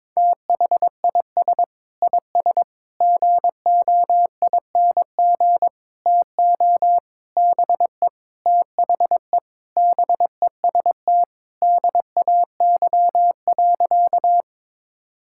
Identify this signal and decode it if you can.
Morse code, heard as a keyed tone, about 21 wpm, 700 Hz. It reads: THIS IS GOING TO BE THE BEST DAY.
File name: practice / 22wpm / Sentences with 4-Word Phrases - 22wpm